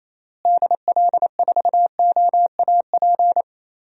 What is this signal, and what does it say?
Morse code, heard as a keyed tone, about 27 wpm, 700 Hz. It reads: DL4OAP
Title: USA Call Signs - 6 Characters - Repeated 3x - 28wpm